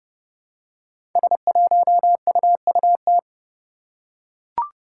Morse code, heard as a keyed tone, about 30 wpm, 700 Hz. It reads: S1UUT
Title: Five Challenging Letter-Number Combinations 30wpm